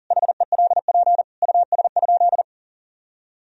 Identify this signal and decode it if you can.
Morse code, heard as a keyed tone, about 40 wpm, 700 Hz. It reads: HELP US?
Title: Sentences from Top 1000 Words - Review Part 3 - 40wpm